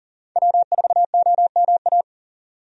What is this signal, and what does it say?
Morse code, heard as a keyed tone, about 40 wpm, 700 Hz. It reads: W4OMA
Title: USA Call Signs - 5 Characters - 40wpm